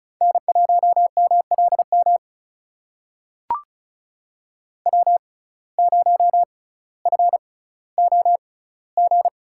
Morse code, heard as a keyed tone, about 36 wpm, spaced as for 18 wpm, 700 Hz. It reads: N1MLM W0FOG
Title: USA Call Signs - 5 Characters - 6x Character Spacing - 35wpm